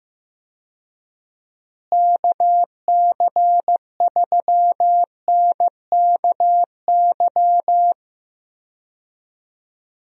Morse code, heard as a keyed tone, about 15 wpm, 700 Hz. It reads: KC3NKY
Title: USA Call Signs - 6 Characters - Repeated 3x - 15wpm